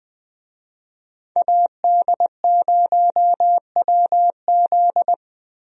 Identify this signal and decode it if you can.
Morse code, heard as a keyed tone, about 21 wpm, 700 Hz. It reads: AD0WZ
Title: USA Call Signs - 5 Characters - Repeated 2x - 20wpm